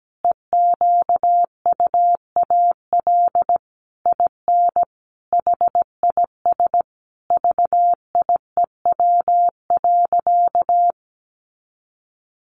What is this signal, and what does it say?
Morse code, heard as a keyed tone, about 17 wpm, 700 Hz. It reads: EQUAL IN HIS VIEW.
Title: Sentences from Top 1400 Words 17wpm